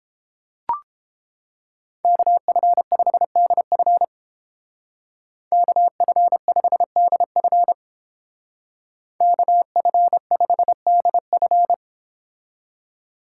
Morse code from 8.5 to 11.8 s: KF5DF

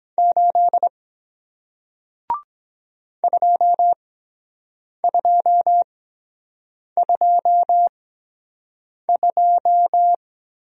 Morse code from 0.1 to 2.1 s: 8

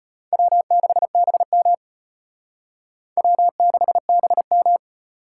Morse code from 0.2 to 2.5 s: W6BM

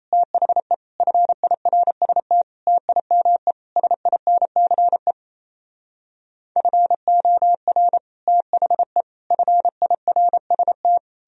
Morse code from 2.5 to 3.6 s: TIME

